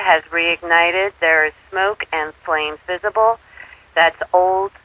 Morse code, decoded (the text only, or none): none